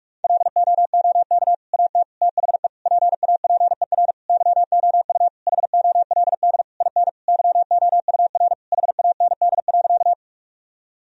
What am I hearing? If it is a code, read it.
Morse code, heard as a keyed tone, about 46 wpm, 700 Hz. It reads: LOOK AT THE PAPER YOU HOLD IN YOUR HAND.
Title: Sentences from Top 1000 Words - Review Part 2 - 45wpm